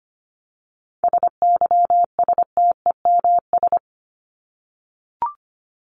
Morse code, heard as a keyed tone, about 25 wpm, 700 Hz. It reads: SYSTEMS
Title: ES:S Suffix - Top 100 Words with Base Words 25wpm